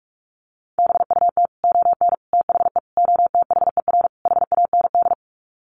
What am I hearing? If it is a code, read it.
Morse code, heard as a keyed tone, about 44 wpm, 700 Hz. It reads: BUT ON THE OTHER HAND
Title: Sets of 5 Words 45wpm